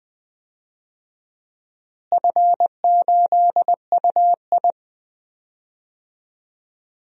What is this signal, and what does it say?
Morse code, heard as a keyed tone, about 21 wpm, 700 Hz. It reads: F8UI
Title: DXCC-100 Call Signs - 4 Characters - Repeated 2x - 20wpm